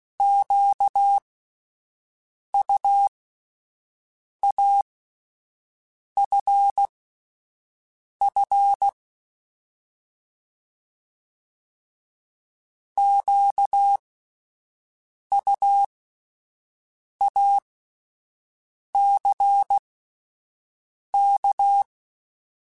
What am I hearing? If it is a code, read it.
Morse code, heard as a keyed tone, about 16 wpm, spaced as for 5 wpm, 800 Hz. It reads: QUAFF QUACK
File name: Alphabetical / Alphabetical mp3s / Qs @5